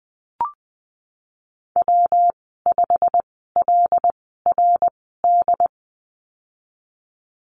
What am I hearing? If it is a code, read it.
Morse code, heard as a keyed tone, about 20 wpm, spaced as for 15 wpm, 700 Hz. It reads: W5LRD
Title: 173 Lesson 28g - Number 9 - Call Signs 20wpm